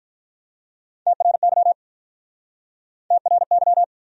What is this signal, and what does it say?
Morse code, heard as a keyed tone, about 48 wpm, 700 Hz. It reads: TRY TRY